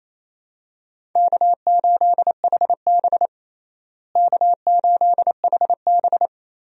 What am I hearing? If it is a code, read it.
Morse code, heard as a keyed tone, about 28 wpm, 700 Hz. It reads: K8HB K8HB
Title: USA Call Signs - 4 Characters - Repeated 2x - 28wpm